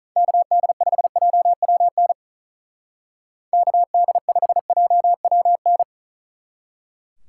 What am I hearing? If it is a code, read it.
Morse code, heard as a keyed tone, about 43 wpm, 700 Hz. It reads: KD5JWN KD5JWN
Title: Speed Racing - USA Call Signs 35wpm